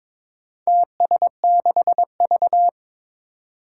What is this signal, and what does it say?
Morse code, heard as a keyed tone, about 22 wpm, 700 Hz. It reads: TS6V
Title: Four Challenging Letter-Number Combinations 22wpm